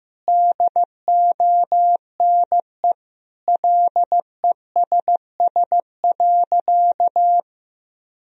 Morse code, heard as a keyed tone, about 15 wpm, 700 Hz. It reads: DONE LESS.
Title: Sentences from Top 1000 Words - Review Part 3 - 15wpm